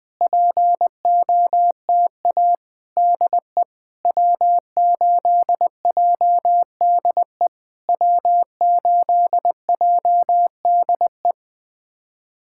Morse code, heard as a keyed tone, about 20 wpm, 700 Hz. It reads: POTA DE W8JDE W8JDE
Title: POTA QSO Practice - 20wpm